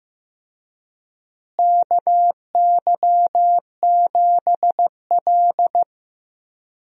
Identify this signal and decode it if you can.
Morse code, heard as a keyed tone, about 15 wpm, 700 Hz. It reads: KY7L
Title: USA Call Signs - 4 Characters - Repeated 2x - 15wpm